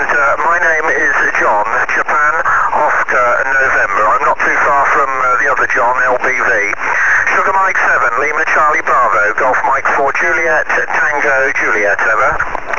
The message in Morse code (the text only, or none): none